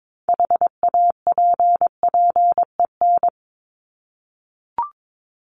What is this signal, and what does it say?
Morse code, heard as a keyed tone, about 22 wpm, 700 Hz. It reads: HAPPEN
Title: ED Suffix - Top 100 Words with Base Words 22wpm